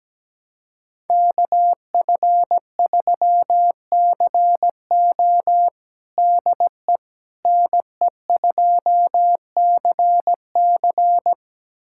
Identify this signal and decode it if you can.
Morse code, heard as a keyed tone, about 17 wpm, 700 Hz. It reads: KF3CO DE NE2CC
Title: USA Callsign DE Callsign 17wpm